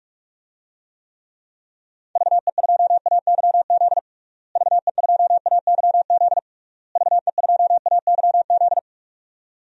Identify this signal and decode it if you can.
Morse code, heard as a keyed tone, about 46 wpm, 700 Hz. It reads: VE2AYZ VE2AYZ VE2AYZ
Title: DXCC-100 Call Signs - 6 Characters - Repeated 3x - 45wpm